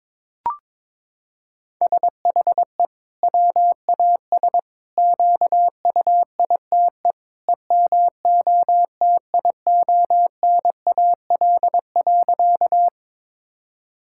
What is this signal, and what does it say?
Morse code, heard as a keyed tone, about 22 wpm, 700 Hz. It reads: SHE WAS QUITE EMOTIONAL.
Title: Sentences from Top 1000 Words - Review Part 5 - 22wpm